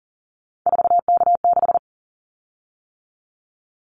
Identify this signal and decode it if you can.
Morse code, heard as a keyed tone, about 40 wpm, 700 Hz. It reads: 4K6
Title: Callsign Prefixes for Top 100 DXCC - 40wpm